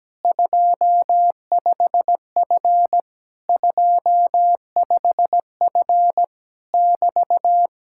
Morse code, heard as a keyed tone, about 17 wpm, 700 Hz. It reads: 25F 25F <BT>
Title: Speed-Racing Common QSO Phrases - 17wpm